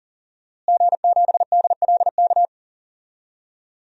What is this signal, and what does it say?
Morse code, heard as a keyed tone, about 40 wpm, 700 Hz. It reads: G7DLK
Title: DXCC-100 Call Signs - 5 Characters - Repeated 2x - 40wpm